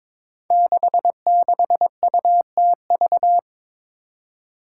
Morse code, heard as a keyed tone, about 22 wpm, 700 Hz. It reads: 66UTV
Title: Five Challenging Letter-Number Combinations 22wpm